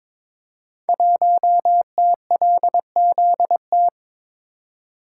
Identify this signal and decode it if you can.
Morse code, heard as a keyed tone, about 22 wpm, 700 Hz. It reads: 1TLZT
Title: Code Groups - 5 Letter:Numbers - 22wpm